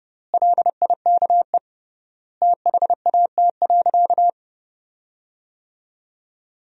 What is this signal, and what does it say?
Morse code, heard as a keyed tone, about 32 wpm, 700 Hz. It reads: LIKE THAT.
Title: 3x Word-Spacing - Sentences from Top 300 Words - 30wpm